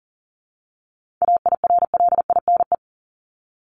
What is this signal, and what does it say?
Morse code, heard as a keyed tone, about 40 wpm, 700 Hz. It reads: AIRLINE
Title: Top 1501-1750 Words 40wpm